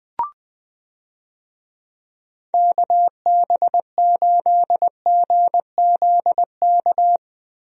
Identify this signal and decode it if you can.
Morse code, heard as a keyed tone, about 21 wpm, 700 Hz. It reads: KB8GZK
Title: USA Call Signs - 6 Characters - Repeated 2x - 20wpm